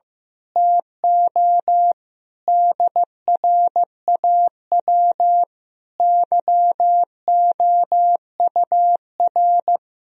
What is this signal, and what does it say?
Morse code, heard as a keyed tone, about 15 wpm, 700 Hz. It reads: TO DRAW YOUR
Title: Sentences from Top 1000 Words - Review Part 1 - 15wpm